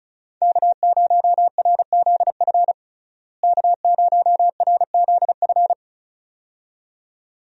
Morse code, heard as a keyed tone, about 36 wpm, 700 Hz. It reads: K0RZF K0RZF
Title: USA Call Signs - 5 Characters - Repeated 2x - 35wpm